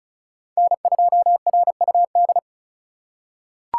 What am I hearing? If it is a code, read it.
Morse code, heard as a keyed tone, about 36 wpm, 700 Hz. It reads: N2RUD
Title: USA Call Signs - 5 Characters - 4x Character Spacing - 35wpm